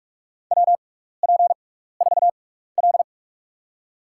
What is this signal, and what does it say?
Morse code, heard as a keyed tone, about 47 wpm, spaced as for 16 wpm, 700 Hz. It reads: WP4L
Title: USA Call Signs - 4 Characters - 6x Character Spacing - 45wpm